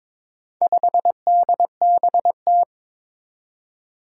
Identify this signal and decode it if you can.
Morse code, heard as a keyed tone, about 22 wpm, 700 Hz. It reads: 5DBT
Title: Four Challenging Letter-Number Combinations 22wpm